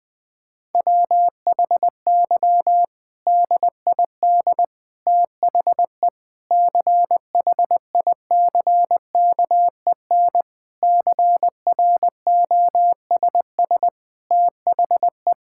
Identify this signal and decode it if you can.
Morse code, heard as a keyed tone, about 21 wpm, 700 Hz. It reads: WHY DID THE CHICKEN CROSS THE
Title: Sentences from Top 1000 Words - Review Part 5 - 20wpm